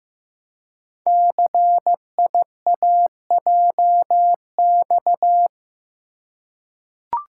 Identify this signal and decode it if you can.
Morse code, heard as a keyed tone, about 15 wpm, 700 Hz. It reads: CIAJX
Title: Code Groups - 5 Letters - 15wpm